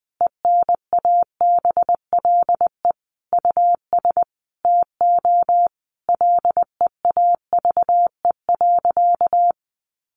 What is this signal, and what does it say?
Morse code, heard as a keyed tone, about 20 wpm, 700 Hz. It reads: ENABLE US TO LEAVE.
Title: Sentences from Top 1500 Words 20wpm